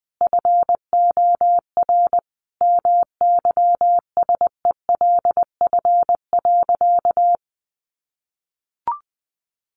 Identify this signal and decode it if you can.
Morse code, heard as a keyed tone, about 20 wpm, 700 Hz. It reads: FOR MYSELF.
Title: Sentences from Top 1300 Words 20wpm